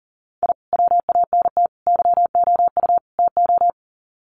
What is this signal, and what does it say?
Morse code, heard as a keyed tone, about 39 wpm, 700 Hz. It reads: I WANT YOU TO
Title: Sets of 4 Words 40wpm